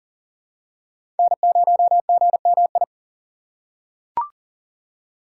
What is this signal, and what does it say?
Morse code, heard as a keyed tone, about 40 wpm, 700 Hz. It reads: N0GMI
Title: USA Call Signs - 5 Characters - 3x Character Spacing - 40wpm